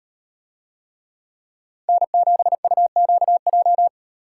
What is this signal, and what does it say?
Morse code, heard as a keyed tone, about 36 wpm, 700 Hz. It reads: N7UQJ